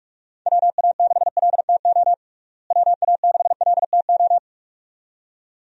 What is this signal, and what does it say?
Morse code, heard as a keyed tone, about 46 wpm, 700 Hz. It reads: WA6LTO WA6LTO
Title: USA Call Signs - 6 Characters - Repeated 2x - 45wpm